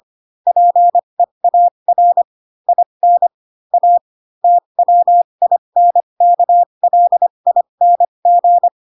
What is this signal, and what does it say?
Morse code, heard as a keyed tone, about 25 wpm, spaced as for 22 wpm, 700 Hz. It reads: PEAR IN A TWINKLING
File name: Pinocchio / Pinocchio ch7 18 25 ik1hge